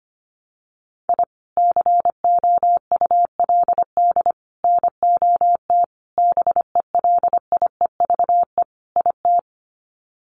Morse code, heard as a keyed tone, about 25 wpm, 700 Hz. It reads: I COULD NOT BELIEVE IT
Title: Sets of 5 Words Encore 25wpm